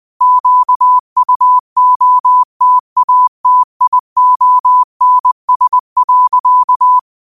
Morse code, heard as a keyed tone, about 20 wpm, 1000 Hz. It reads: QUOTATIONS.
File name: qotdfiles / 20wpm